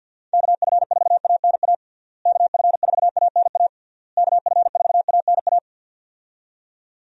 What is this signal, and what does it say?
Morse code, heard as a keyed tone, about 51 wpm, 700 Hz. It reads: KF4ANA KF4ANA KF4ANA
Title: USA Call Signs - 6 Characters - Repeated 3x - 50wpm